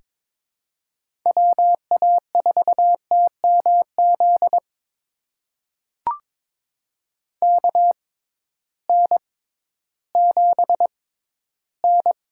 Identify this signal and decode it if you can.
Morse code, heard as a keyed tone, about 23 wpm, spaced as for 13 wpm, 700 Hz. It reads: WA4TMZ KN7N
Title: USA Call Signs - 6x Character Spacing - 22wpm